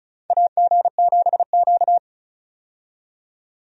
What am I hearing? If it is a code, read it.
Morse code, heard as a keyed tone, about 36 wpm, 700 Hz. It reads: AG7Q
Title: USA Call Signs - 4 Characters - 35wpm